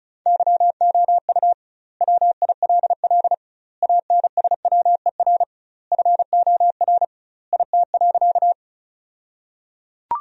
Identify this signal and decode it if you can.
Morse code, heard as a keyed tone, about 36 wpm, 700 Hz. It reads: YOU WILL ANSWER FOR IT.
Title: 2x Word-Spacing - Sentences from Top 300 Words - 35wpm